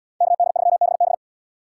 Morse code, heard as a keyed tone, about 68 wpm, 700 Hz. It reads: KD3BB